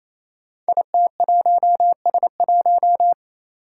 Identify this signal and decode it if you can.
Morse code, heard as a keyed tone, about 27 wpm, 700 Hz. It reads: IT1S1
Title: Five Challenging Letter-Number Combinations 28wpm